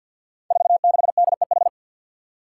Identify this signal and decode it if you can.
Morse code, heard as a keyed tone, about 53 wpm, 700 Hz. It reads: 46DEH
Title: Five Challenging Letter-Number Combinations 50wpm